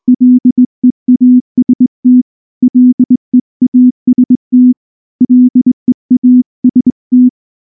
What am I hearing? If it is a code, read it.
Morse code, heard as a keyed tone, about 20 wpm, 250 Hz. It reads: LEAST LEAST LEAST
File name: morse / morse-training-20WPM-3